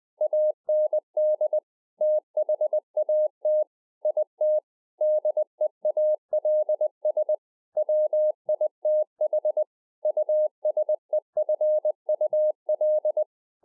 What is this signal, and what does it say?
Morse code, heard as a keyed tone, about 20 wpm, 610 Hz. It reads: AND THAT IT DEALS WITH USEFUL
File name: 2011-04-mp3 / A SMALL TRANCEIVER 3 20WPM